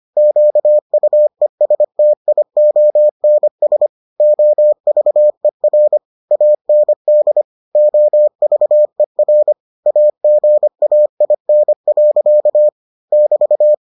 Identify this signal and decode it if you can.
Morse code, heard as a keyed tone, about 25 wpm, 600 Hz. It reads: QUESTIONS OVER AND OVER AGAIN. <BT>